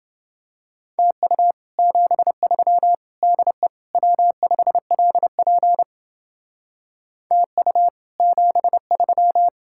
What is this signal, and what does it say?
Morse code, heard as a keyed tone, about 32 wpm, 700 Hz. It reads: TU 73 DE W5LP TU 73